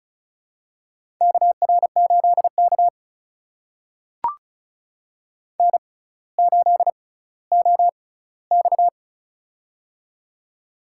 Morse code, heard as a keyed tone, about 36 wpm, spaced as for 18 wpm, 700 Hz. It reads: KR8K N8OX
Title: USA Call Signs - 6x Character Spacing - 35wpm